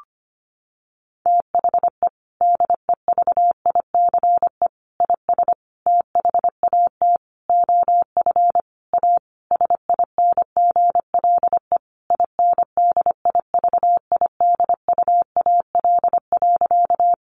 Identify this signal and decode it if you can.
Morse code, heard as a keyed tone, about 25 wpm, 700 Hz. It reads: THE DEVICE IS THAT OF A SINGLE INDIVIDUAL.
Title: Sentences from Top 1500 Words 25wpm